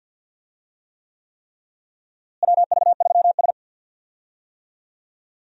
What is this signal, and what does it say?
Morse code, heard as a keyed tone, about 47 wpm, 700 Hz. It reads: WV3S
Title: USA Call Signs - 4 Characters - Repeated 3x - 50wpm